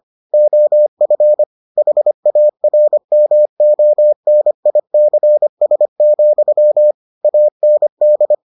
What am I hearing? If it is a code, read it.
Morse code, heard as a keyed tone, about 25 wpm, 600 Hz. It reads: OF HARMONICS, AND